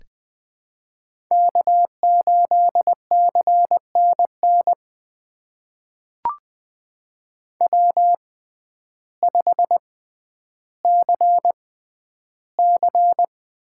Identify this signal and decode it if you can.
Morse code, heard as a keyed tone, about 20 wpm, spaced as for 11 wpm, 700 Hz. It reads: K8CNN W5CC
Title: USA Call Signs - 6x Character Spacing - 20wpm